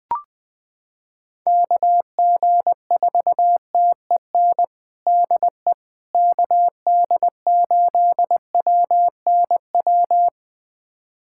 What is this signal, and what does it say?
Morse code, heard as a keyed tone, about 20 wpm, 700 Hz. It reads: KG4TEN DE KD8WNW
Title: USA Callsign DE Callsign - 6 Characters - 20wpm